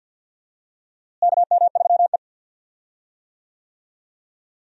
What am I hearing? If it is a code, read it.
Morse code, heard as a keyed tone, about 52 wpm, 700 Hz. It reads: KM3E
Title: USA Call Signs - 4 Characters - Repeated 3x - 50wpm